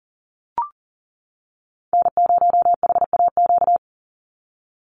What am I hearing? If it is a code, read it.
Morse code, heard as a keyed tone, about 40 wpm, 700 Hz. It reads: N0HAQ
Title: USA Call Signs 40wpm